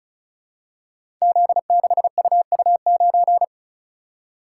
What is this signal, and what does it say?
Morse code, heard as a keyed tone, about 36 wpm, 700 Hz. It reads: Z6UU9
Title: Code Groups - 5 Letter:Numbers - 35wpm